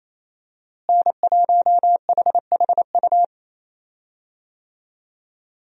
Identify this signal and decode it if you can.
Morse code, heard as a keyed tone, about 28 wpm, 700 Hz. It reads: N1HHU
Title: USA Call Signs - 5 Characters - Repeated 2x - 28wpm